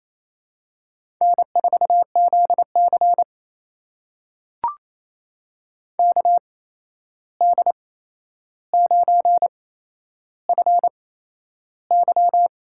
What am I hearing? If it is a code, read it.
Morse code, heard as a keyed tone, about 28 wpm, spaced as for 11 wpm, 700 Hz. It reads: N4ZC KD9FY
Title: USA Call Signs - 8x Character Spacing - 28wpm